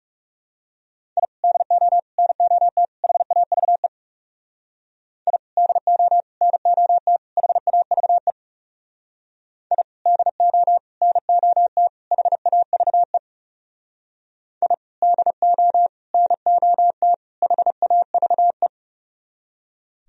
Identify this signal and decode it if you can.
Morse code, heard as a keyed tone, about 38 wpm, 700 Hz. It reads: I DO NOT HAVE I DO NOT HAVE I DO NOT HAVE I DO NOT HAVE